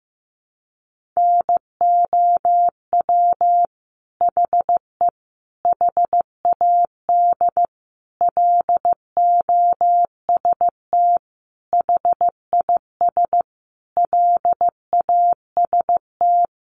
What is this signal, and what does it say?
Morse code, heard as a keyed tone, about 15 wpm, 700 Hz. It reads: NOW HE HAD LOST HIS LAST
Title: Sentences from Top 1400 Words 15wpm